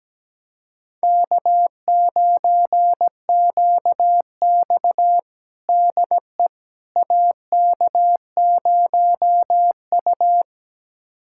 Morse code, heard as a keyed tone, about 17 wpm, 700 Hz. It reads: K9QX DE AK0U
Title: USA Callsign DE Callsign - 4 Characters - 17wpm